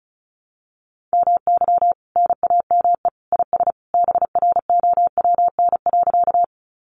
Morse code, heard as a keyed tone, about 34 wpm, 700 Hz. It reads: MY NAME IS BROWN.
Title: Sentences from Top 1200 Words 35wpm